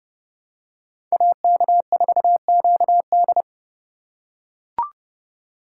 Morse code, heard as a keyed tone, about 31 wpm, 700 Hz. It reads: AK4QD
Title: USA Call Signs - 6x Character Spacing - 30wpm